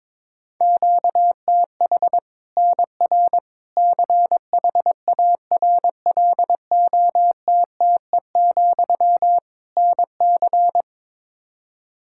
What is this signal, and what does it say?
Morse code, heard as a keyed tone, about 21 wpm, 700 Hz. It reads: QTH NR CHARLOTTE, NC
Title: Practice QSO - Exchange 1 - Beginning - 22wpm